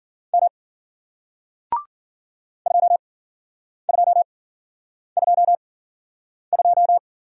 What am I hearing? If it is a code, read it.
Morse code, heard as a keyed tone, about 55 wpm, spaced as for 9 wpm, 700 Hz. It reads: M 2222